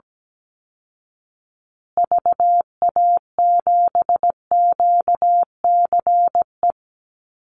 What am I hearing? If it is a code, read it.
Morse code, heard as a keyed tone, about 17 wpm, 700 Hz. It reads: VA7QCE
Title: Call Signs Repeated 3x - 17wpm